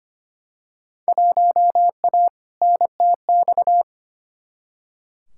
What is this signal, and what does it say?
Morse code, heard as a keyed tone, about 25 wpm, 700 Hz. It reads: 1A NTX
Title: Field Day Exchanges - 25wpm